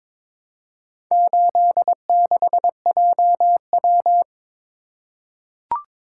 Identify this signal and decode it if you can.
Morse code, heard as a keyed tone, about 23 wpm, 700 Hz. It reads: 86JW
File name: practice / 22wpm / Four Letter-Number Combinations - 22wpm